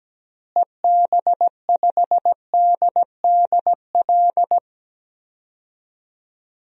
Morse code, heard as a keyed tone, about 17 wpm, 700 Hz. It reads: EB5DDL
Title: DXCC-100 Call Signs - 6 Characters - Repeated 2x - 17wpm